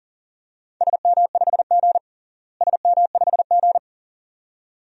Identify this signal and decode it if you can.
Morse code, heard as a keyed tone, about 40 wpm, 700 Hz. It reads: SM5G SM5G
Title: DXCC-100 Call Signs - 4 Characters - Repeated 2x - 40wpm